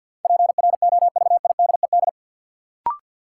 Morse code, heard as a keyed tone, about 51 wpm, 700 Hz. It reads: PROVIDED
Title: Sentences from Top 1500 Words - New Word Drill Alt - 50wpm